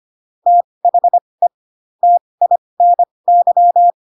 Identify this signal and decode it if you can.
Morse code, heard as a keyed tone, about 25 wpm, spaced as for 20 wpm, 700 Hz. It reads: THE TINY
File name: Pinocchio / Pinocchio ch1 15 25 ik1hge